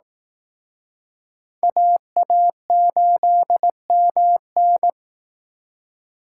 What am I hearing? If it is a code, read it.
Morse code, heard as a keyed tone, about 20 wpm, 700 Hz. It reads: AA8MN